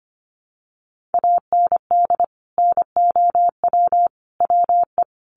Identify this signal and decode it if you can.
Morse code, heard as a keyed tone, about 25 wpm, 700 Hz. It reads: AND NOW WE
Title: Sentences from Top 1200 Words 25wpm